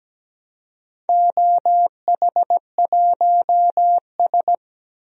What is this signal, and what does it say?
Morse code, heard as a keyed tone, about 17 wpm, 700 Hz. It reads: OH1S
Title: DXCC-100 Call Signs - 4 Characters - Repeated 3x - 17wpm